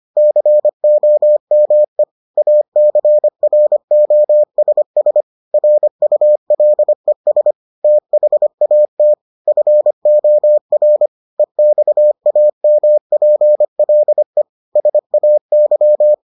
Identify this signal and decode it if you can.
Morse code, heard as a keyed tone, about 25 wpm, 600 Hz. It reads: COME ACROSS RULES THAT FOR EXAMPLE SAY